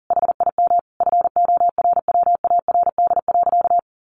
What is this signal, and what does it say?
Morse code, heard as a keyed tone, about 38 wpm, 700 Hz. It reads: HIM FORWARD.
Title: Sentences from Top 1300 Words 40wpm